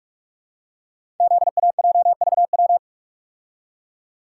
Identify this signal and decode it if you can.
Morse code, heard as a keyed tone, about 44 wpm, 700 Hz. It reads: ZAJVW
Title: Code Groups - 5 Letters - 45wpm